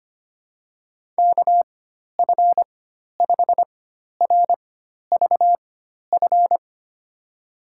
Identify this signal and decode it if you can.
Morse code, heard as a keyed tone, about 25 wpm, spaced as for 12 wpm, 700 Hz. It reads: KF5RVF
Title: USA Call Signs - 6 Characters - 4x Character Spacing - 25wpm